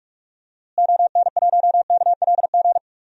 Morse code, heard as a keyed tone, about 43 wpm, 700 Hz. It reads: ON1KLG
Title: DXCC-100 Call Signs - 6 Characters - Repeated 3x - 45wpm